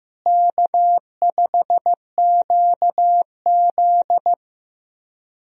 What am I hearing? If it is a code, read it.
Morse code, heard as a keyed tone, about 15 wpm, 700 Hz. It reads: K5QZ
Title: USA Call Signs - 4 Characters - Repeated 2x - 15wpm